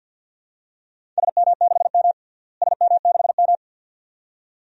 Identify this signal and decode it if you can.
Morse code, heard as a keyed tone, about 50 wpm, 700 Hz. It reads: SM6M SM6M
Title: DXCC-100 Call Signs - 4 Characters - Repeated 2x - 50wpm